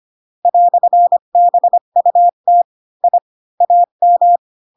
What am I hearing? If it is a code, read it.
Morse code, heard as a keyed tone, about 25 wpm, 700 Hz. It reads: "BUT I AM
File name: Pinocchio / Pinocchio ch8 20 25 ik1hge